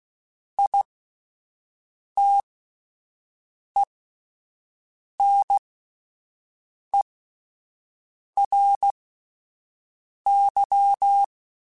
Morse code, heard as a keyed tone, about 16 wpm, spaced as for 5 wpm, 800 Hz. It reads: ITENERY
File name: Alphabetical / Alphabetical mp3s / Is @5